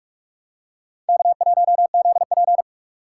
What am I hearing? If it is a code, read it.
Morse code, heard as a keyed tone, about 46 wpm, 700 Hz. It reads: K1ZP
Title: USA Call Signs - 4 Characters - 6x Character Spacing - 45wpm